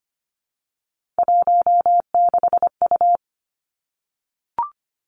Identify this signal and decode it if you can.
Morse code, heard as a keyed tone, about 25 wpm, 700 Hz. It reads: 16U
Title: Three Challenging Letter-Number Combinations 25wpm